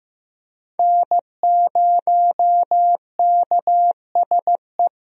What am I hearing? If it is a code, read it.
Morse code, heard as a keyed tone, about 16 wpm, 700 Hz. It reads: N0KSE
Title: USA Call Signs - 5 Characters - Repeated 2x - 15wpm